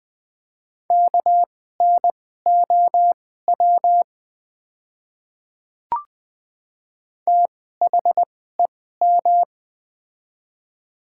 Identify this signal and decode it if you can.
Morse code, heard as a keyed tone, about 20 wpm, spaced as for 14 wpm, 700 Hz. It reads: KNOW THEM
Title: 2x Character Spacing - Top 100 Words - 20wpm